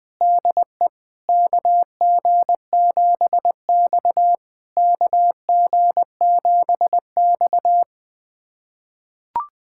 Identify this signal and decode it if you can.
Morse code, heard as a keyed tone, about 20 wpm, 660 Hz. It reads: DE KG7X KG7X
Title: SOTA QSO Practice 20wpm